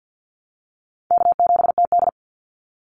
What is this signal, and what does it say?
Morse code, heard as a keyed tone, about 50 wpm, 700 Hz. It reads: K7TD
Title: Easy Call Signs Repeated 3x - 50wpm